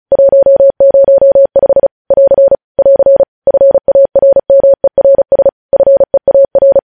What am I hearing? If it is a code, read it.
Morse code, heard as a keyed tone, about 36 wpm, 550 Hz. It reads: 105 <AR> <AR> FARMERS FEAR
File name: podcast / News headlines-1x-35x35